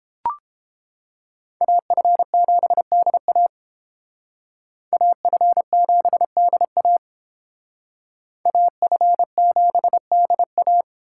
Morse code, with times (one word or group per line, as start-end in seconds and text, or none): 1.6-3.5 AF7DA
4.9-7.0 AF7DA
8.5-10.8 AF7DA